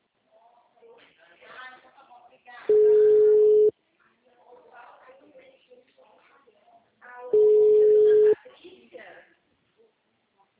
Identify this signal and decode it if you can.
Ringback tone (Europe, 425 Hz), with speech.
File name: out- -2010- - - .16243